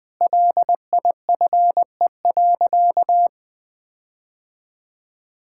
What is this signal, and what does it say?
Morse code, heard as a keyed tone, about 20 wpm, 700 Hz. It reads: LIFE.
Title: Sentences with 6-Word Phrases - Spoken Beginning - 20wpm